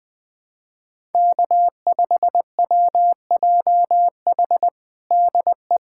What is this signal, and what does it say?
Morse code, heard as a keyed tone, about 20 wpm, 700 Hz. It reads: K5WJH DE
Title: USA Callsign DE Callsign - 5 Characters - 20wpm